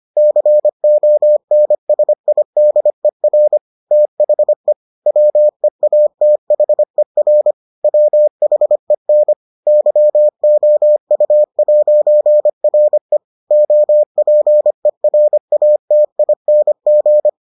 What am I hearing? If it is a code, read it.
Morse code, heard as a keyed tone, about 25 wpm, 600 Hz. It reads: CONSIDER THE WEATHER WHEN YOU'RE OPERATING